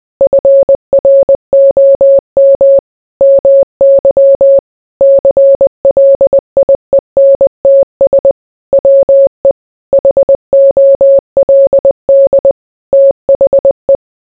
Morse code, heard as a keyed tone, about 20 wpm, 550 Hz. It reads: FROM MY CLIENTS WE HOLD THE